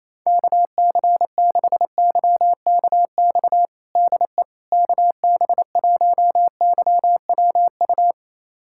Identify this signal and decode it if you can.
Morse code, heard as a keyed tone, about 28 wpm, 700 Hz. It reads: KC6YKX DE KB1YWU
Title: USA Callsign DE Callsign - 6 Characters - 28wpm